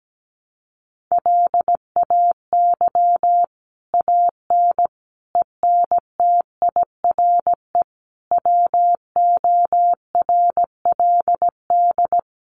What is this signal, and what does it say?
Morse code, heard as a keyed tone, about 17 wpm, 700 Hz. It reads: LAY AN ENTIRE WORLD
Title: A Sound Of Thunder 17wpm